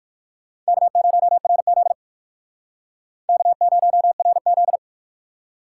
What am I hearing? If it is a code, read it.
Morse code, heard as a keyed tone, about 51 wpm, 700 Hz. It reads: K0RZ K0RZ
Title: Speed Racing - USA 1x2 Call Signs 45wpm